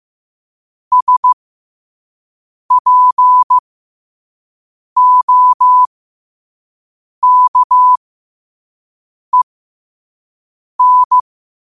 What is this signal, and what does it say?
Morse code, heard as a keyed tone, about 15 wpm, spaced as for 5 wpm, 1000 Hz. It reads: SPOKEN
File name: qotdfiles / 5wpm